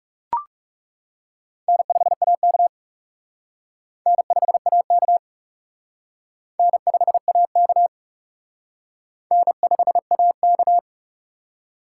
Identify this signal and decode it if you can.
Morse code, heard as a keyed tone, about 35 wpm, 700 Hz. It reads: N5AK N5AK N5AK N5AK
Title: Speed Racing - USA 1x2 Call Signs 30wpm